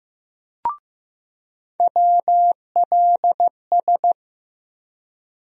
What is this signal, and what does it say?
Morse code, heard as a keyed tone, about 15 wpm, 700 Hz. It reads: WLS
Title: USA Call Sign Postfixes - 3 Letters 15wpm